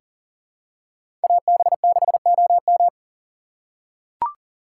Morse code, heard as a keyed tone, about 40 wpm, 700 Hz. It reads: AB6OM
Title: USA Call Signs - 5 Characters - 4x Character Spacing - 40wpm